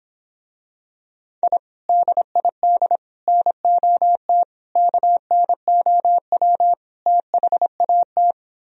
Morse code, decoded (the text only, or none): I DID NOT KNOW THAT